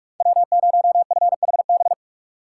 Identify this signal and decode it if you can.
Morse code, heard as a keyed tone, about 42 wpm, 700 Hz. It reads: W0FHB
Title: USA Call Signs - 5 Characters - 6x Character Spacing - 45wpm